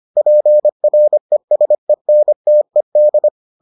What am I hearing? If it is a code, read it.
Morse code, heard as a keyed tone, about 25 wpm, 600 Hz. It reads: PRESENTED